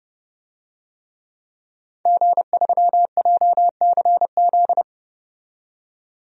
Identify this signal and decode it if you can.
Morse code, heard as a keyed tone, about 30 wpm, 700 Hz. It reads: G3JCZ
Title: DXCC-100 Call Signs - 5 Characters - 30wpm